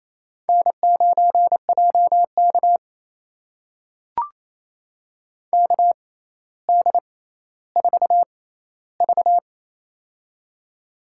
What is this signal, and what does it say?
Morse code, heard as a keyed tone, about 29 wpm, spaced as for 15 wpm, 700 Hz. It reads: N9JK KD4V
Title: USA Call Signs - 4 Characters - 6x Character Spacing - 28wpm